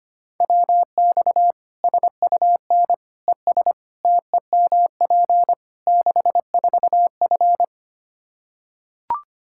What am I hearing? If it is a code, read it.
Morse code, heard as a keyed tone, about 25 wpm, 700 Hz. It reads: WX SUN ES TEMP 64F
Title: Practice QSO - Exchange 2 - Info - 25wpm